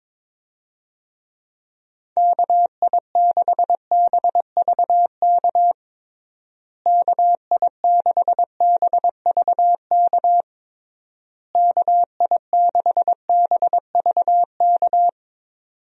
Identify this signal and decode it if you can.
Morse code, heard as a keyed tone, about 22 wpm, 700 Hz. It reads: KI6BVK KI6BVK KI6BVK
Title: USA Call Signs - 6 Characters - Repeated 3x - 22wpm